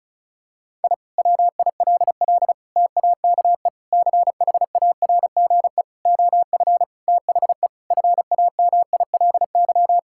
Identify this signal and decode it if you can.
Morse code, heard as a keyed tone, about 36 wpm, 700 Hz. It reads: I WILL TAKE CHARGE OF THE FAMILY
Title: Sentences from Top 1000 Words - Review Part 6 - 35wpm